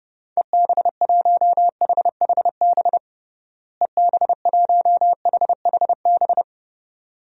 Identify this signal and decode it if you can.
Morse code, heard as a keyed tone, about 30 wpm, 700 Hz. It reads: EB1HHB EB1HHB
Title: DXCC-100 Call Signs - 6 Characters - Repeated 2x - 30wpm